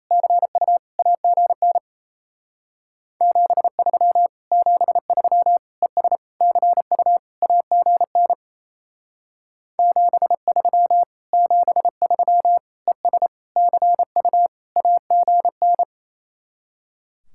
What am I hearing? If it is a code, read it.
Morse code, heard as a keyed tone, about 37 wpm, 700 Hz. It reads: CU AGN 73 73 ES CU AGN 73 73 ES CU AGN